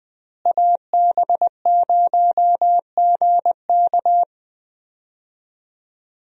Morse code, heard as a keyed tone, about 21 wpm, 700 Hz. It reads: AB0GK
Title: USA Call Signs - 5 Characters - Repeated 2x - 20wpm